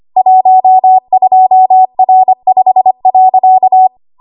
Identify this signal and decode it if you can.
Morse code, heard as a keyed tone, about 25 wpm, 750 Hz. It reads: 12R5.
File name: Archive / 25 WPM / 25WPM